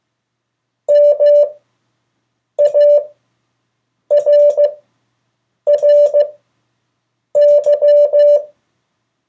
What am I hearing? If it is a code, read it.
Morse code, heard as a keyed tone, about 15 wpm, spaced as for 6 wpm, 580 Hz. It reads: MARRY
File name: 2of4-words-1